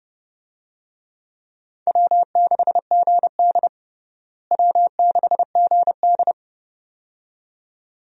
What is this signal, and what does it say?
Morse code, heard as a keyed tone, about 30 wpm, 700 Hz. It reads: W6GD W6GD
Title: USA Call Signs - 4 Characters - Repeated 2x - 30wpm